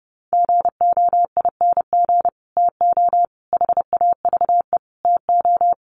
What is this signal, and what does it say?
Morse code, heard as a keyed tone, about 30 wpm, 700 Hz. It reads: GOING TO HAVE TO
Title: Sets of 5 Words Encore 30wpm